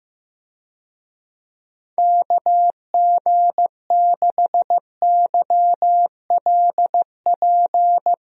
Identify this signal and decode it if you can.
Morse code, heard as a keyed tone, about 15 wpm, 700 Hz. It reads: KG6YLP
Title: USA Call Signs - 6 Characters - Repeated 3x - 15wpm